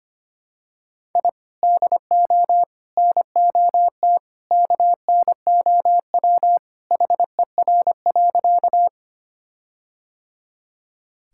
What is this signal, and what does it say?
Morse code, heard as a keyed tone, about 25 wpm, 700 Hz. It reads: I DO NOT KNOW HER.
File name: practice / 25wpm / Sentences with 4-Word Phrases - 25wpm